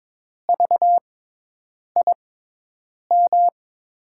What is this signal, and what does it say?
Morse code, heard as a keyed tone, about 22 wpm, spaced as for 8 wpm, 700 Hz. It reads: VIM
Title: USA Call Signs - 5 Characters - 6x Character Spacing - 22wpm